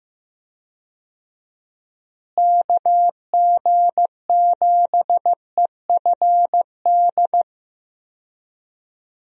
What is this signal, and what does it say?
Morse code, heard as a keyed tone, about 15 wpm, 700 Hz. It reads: KG7EFD
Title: USA Call Signs - 6 Characters - Repeated 3x - 15wpm